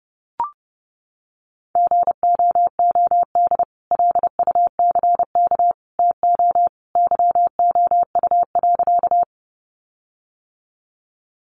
Morse code, heard as a keyed tone, about 30 wpm, 700 Hz. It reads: GOOD LUCK TO YOU.
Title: Sentences from Top 1500 Words 30wpm